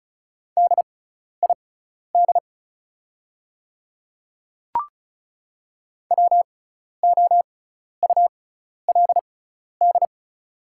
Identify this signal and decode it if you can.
Morse code, heard as a keyed tone, about 36 wpm, spaced as for 12 wpm, 700 Hz. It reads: DID WOULD
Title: 6x Character Spacing - Top 100 Words - 35wpm